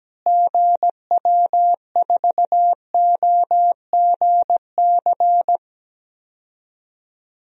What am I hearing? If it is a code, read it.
Morse code, heard as a keyed tone, about 17 wpm, 700 Hz. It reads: GW4OGC
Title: DXCC-100 Call Signs - 6 Characters - Repeated 3x - 17wpm